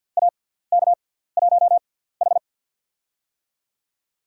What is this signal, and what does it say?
Morse code, heard as a keyed tone, about 54 wpm, spaced as for 17 wpm, 690 Hz. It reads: AK1H
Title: USA Call Signs - 4 Characters - 6x Character Spacing - 50wpm